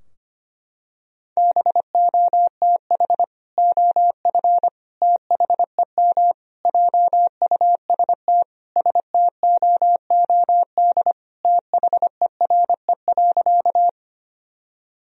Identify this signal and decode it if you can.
Morse code, heard as a keyed tone, about 25 wpm, 700 Hz. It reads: BOTH OF THEM JUST STOOD THERE.
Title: Sentences from Top 1000 Words - Review Part 2 - 25wpm